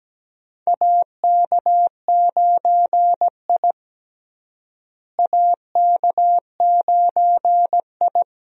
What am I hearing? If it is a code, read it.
Morse code, heard as a keyed tone, about 17 wpm, 700 Hz. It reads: AK9I AK9I
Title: USA Call Signs - 4 Characters - Repeated 3x - 17wpm